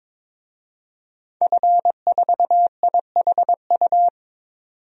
Morse code, heard as a keyed tone, about 22 wpm, 700 Hz. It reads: F4IHU
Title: DXCC-100 Call Signs - 5 Characters - 22wpm